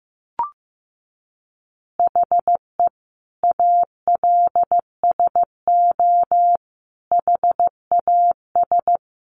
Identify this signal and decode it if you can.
Morse code, heard as a keyed tone, about 15 wpm, 700 Hz. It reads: HE ALSO HAS
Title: Additional Easy Phrases 15wpm